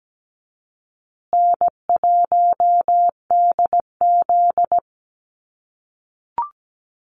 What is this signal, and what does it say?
Morse code, heard as a keyed tone, about 17 wpm, 700 Hz. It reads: N1DZ
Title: USA Call Signs Repeated 2x 17wpm